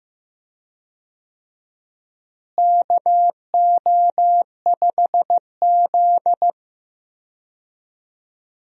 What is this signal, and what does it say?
Morse code, heard as a keyed tone, about 15 wpm, 700 Hz. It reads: KO5Z
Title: USA Call Signs - 4 Characters - Repeated 3x - 15wpm